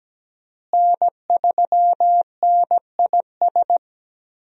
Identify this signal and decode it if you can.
Morse code, heard as a keyed tone, about 17 wpm, 700 Hz. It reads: N3NIS
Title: USA Call Signs - 5 Characters - Repeated 3x - 17wpm